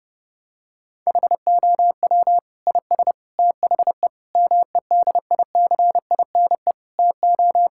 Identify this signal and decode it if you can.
Morse code, heard as a keyed tone, about 32 wpm, 700 Hz. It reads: HOW IS THE MEDICINE TO
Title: Sentences from Top 1000 Words - Review Part 6 - 30wpm